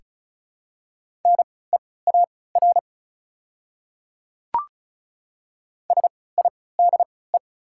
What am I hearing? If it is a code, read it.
Morse code, heard as a keyed tone, about 34 wpm, spaced as for 20 wpm, 690 Hz. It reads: NEAR SIDE
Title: 3x Character Spacing - Top 200 Words - 35wpm